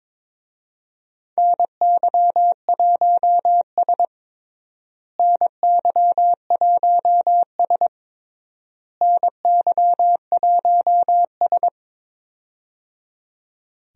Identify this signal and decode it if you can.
Morse code, heard as a keyed tone, about 22 wpm, 700 Hz. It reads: NY1S NY1S NY1S
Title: USA Call Signs - 4 Characters - Repeated 3x - 22wpm